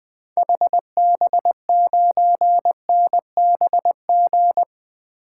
Morse code, heard as a keyed tone, about 20 wpm, 700 Hz. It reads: HB9NBG
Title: 2023 SOTA Activator Call Signs - 20wpm